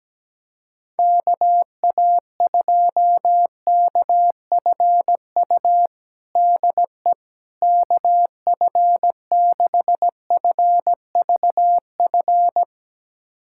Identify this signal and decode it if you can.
Morse code, heard as a keyed tone, about 17 wpm, 700 Hz. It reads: KA2KFU DE KF6FVF
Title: USA Callsign DE Callsign - 6 Characters - 17wpm